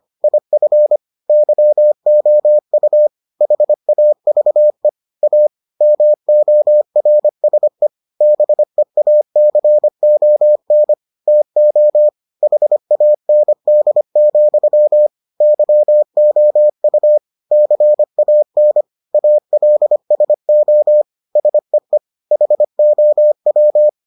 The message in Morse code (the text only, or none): IF YOU HAVE A MORSE BEACON TO HAND, YOU CAN ALSO SEE HOW